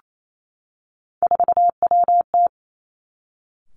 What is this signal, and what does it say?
Morse code, heard as a keyed tone, about 28 wpm, 700 Hz. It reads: 4WT
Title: Three Challenging Letter-Number Combinations 28wpm